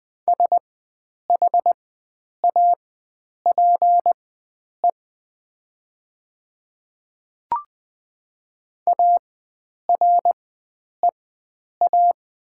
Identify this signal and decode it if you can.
Morse code, heard as a keyed tone, about 20 wpm, spaced as for 9 wpm, 700 Hz. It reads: SHAPE AREA
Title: 4x Character Spacing - Top 500 Words - 20wpm